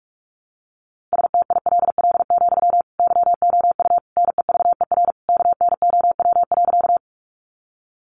Morse code, heard as a keyed tone, about 45 wpm, 700 Hz. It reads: STILL, YOU NEVER KNOW.
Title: Sentences from Top 300 Words 45wpm